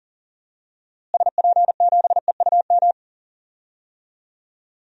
Morse code, heard as a keyed tone, about 40 wpm, 700 Hz. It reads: SP7EUM
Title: DXCC-100 Call Signs - 6 Characters - Repeated 2x - 40wpm